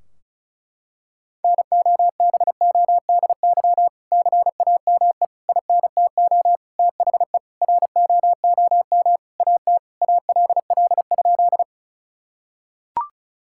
Morse code, heard as a keyed tone, about 36 wpm, 700 Hz. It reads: NOBODY CAME INTO THE ROOM AT ALL?
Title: Sentences from Top 1600 Words 35wpm